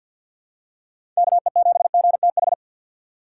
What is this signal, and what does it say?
Morse code, heard as a keyed tone, about 47 wpm, 700 Hz. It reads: KE7GTH
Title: USA Call Signs - 6 Characters - 6x Character Spacing - 50wpm